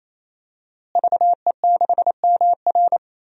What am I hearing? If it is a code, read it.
Morse code, heard as a keyed tone, about 28 wpm, 700 Hz. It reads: VE6MR
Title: DXCC-100 Call Signs - 5 Characters - Repeated 3x - 28wpm